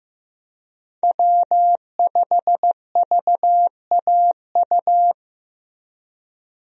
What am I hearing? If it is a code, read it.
Morse code, heard as a keyed tone, about 15 wpm, 700 Hz. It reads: W5VAU
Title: USA Call Signs - 5 Characters - 15wpm